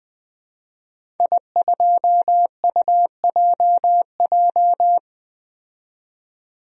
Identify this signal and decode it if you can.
Morse code, heard as a keyed tone, about 20 wpm, 700 Hz. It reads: I2UJJ
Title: DXCC-100 Call Signs - 5 Characters - 20wpm